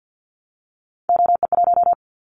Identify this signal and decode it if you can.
Morse code, heard as a keyed tone, about 50 wpm, 700 Hz. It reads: OE1
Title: Callsign Prefixes for Top 25 DXCC - 50wpm